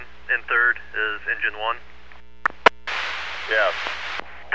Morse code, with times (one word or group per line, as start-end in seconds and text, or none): none